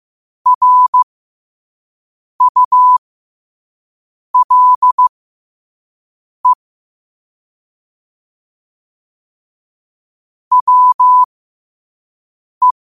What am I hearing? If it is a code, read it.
Morse code, heard as a keyed tone, about 15 wpm, spaced as for 5 wpm, 1000 Hz. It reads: RULE WE
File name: qotdfiles / 5wpm